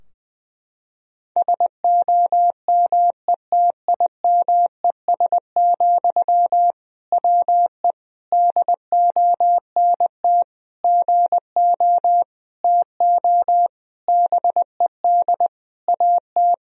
Morse code, heard as a keyed tone, about 20 wpm, 700 Hz. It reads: SOMETIMES, WE DONT GO TO BED AT
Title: Sentences from Top 1000 Words - Review Part 3 - 20wpm